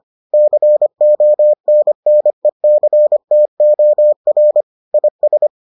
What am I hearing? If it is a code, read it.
Morse code, heard as a keyed tone, about 25 wpm, 600 Hz. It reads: CONNECTOR IS